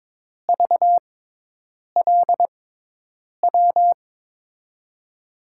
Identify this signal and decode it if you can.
Morse code, heard as a keyed tone, about 22 wpm, spaced as for 8 wpm, 700 Hz. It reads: VLW
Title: USA Call Signs - 6 Characters - 6x Character Spacing - 22wpm